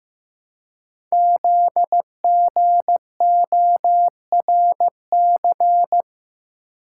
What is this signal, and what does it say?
Morse code, heard as a keyed tone, about 15 wpm, 700 Hz. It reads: ZGORC
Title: Code Groups - 5 Letters - 15wpm